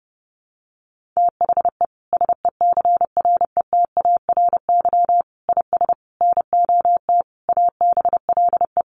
Morse code, heard as a keyed tone, about 30 wpm, 700 Hz. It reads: THE SECRETARY IS NOT ABLE
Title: Sentences from Top 1300 Words 30wpm